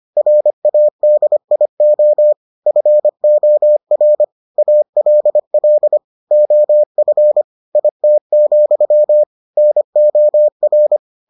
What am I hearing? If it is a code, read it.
Morse code, heard as a keyed tone, about 25 wpm, 600 Hz. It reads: RADIO FOR ALL OF IT, NOR